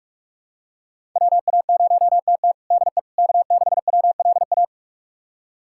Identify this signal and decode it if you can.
Morse code, heard as a keyed tone, about 44 wpm, 700 Hz. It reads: WA0TT DE K6WLA
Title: USA Callsign DE Callsign - 5 Characters - 45wpm